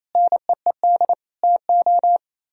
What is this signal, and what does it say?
Morse code, heard as a keyed tone, about 31 wpm, 700 Hz. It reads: NEED TO
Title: Speed-Racing - Sets of 5 Words 28wpm